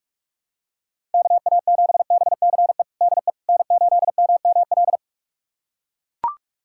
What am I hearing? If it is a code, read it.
Morse code, heard as a keyed tone, about 44 wpm, 700 Hz. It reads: KA7BCE DE N8MML
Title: USA Callsign DE Callsign 45wpm